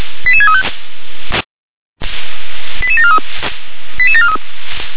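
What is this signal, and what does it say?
Five-tone selective call (ZVEI), heard as quick runs of five tones, tones of about 70 ms. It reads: 88635 88743 88743
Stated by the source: ZVEI five-tone selective call received by SDR.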